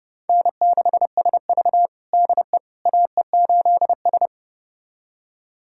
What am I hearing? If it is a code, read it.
Morse code, heard as a keyed tone, about 32 wpm, 700 Hz. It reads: N6SV DE AE8S
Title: USA Callsign DE Callsign - 4 Characters - 30wpm